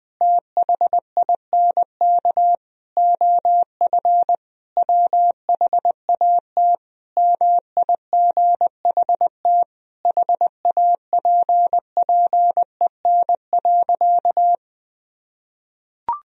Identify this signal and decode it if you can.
Morse code, heard as a keyed tone, about 21 wpm, 700 Hz. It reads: THINK OF WHAT MIGHT HAPPEN.
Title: Sentences from Top 400 Words and Abbreviations 20wpm